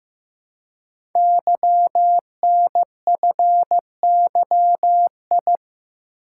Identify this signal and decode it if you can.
Morse code, heard as a keyed tone, about 15 wpm, 700 Hz. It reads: YNFYI
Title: Code Groups - 5 Letters - 15wpm